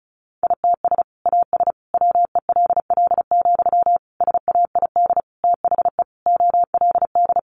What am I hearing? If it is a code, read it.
Morse code, heard as a keyed tone, about 34 wpm, 700 Hz. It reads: ITS AS WELL, SAID THE OLD
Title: Sentences from Top 600 Words 35wpm